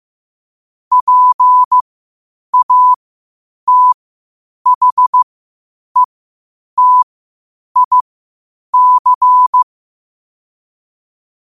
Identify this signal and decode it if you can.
Morse code, heard as a keyed tone, about 15 wpm, spaced as for 8 wpm, 1000 Hz. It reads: PATHETIC
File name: 8wpm